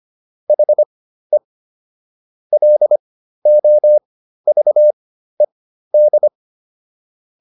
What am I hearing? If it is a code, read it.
Morse code, heard as a keyed tone, about 25 wpm, spaced as for 13 wpm, 600 Hz. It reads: HE LOVED